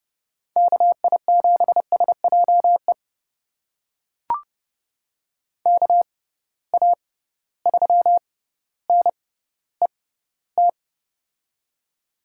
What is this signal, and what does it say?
Morse code, heard as a keyed tone, about 31 wpm, spaced as for 15 wpm, 700 Hz. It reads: KI7SJE KA3NET
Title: USA Call Signs - 6 Characters - 6x Character Spacing - 30wpm